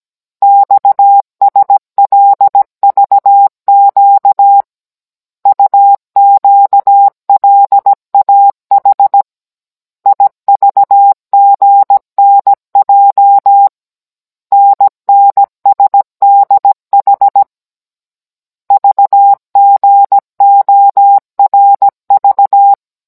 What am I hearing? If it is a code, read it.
Morse code, heard as a keyed tone, about 17 wpm, 800 Hz. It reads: XSLVQ UQLAH IVGNJ NNSDH VGORV
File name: Cours-NR18-16mm